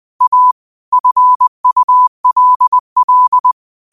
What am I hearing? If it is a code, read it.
Morse code, heard as a keyed tone, about 20 wpm, 1000 Hz. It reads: A FULL
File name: qotdfiles / 20wpm